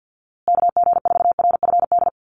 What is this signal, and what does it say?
Morse code, heard as a keyed tone, about 50 wpm, 700 Hz. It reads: KG4RFD
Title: USA Call Signs Repeated 3x 50wpm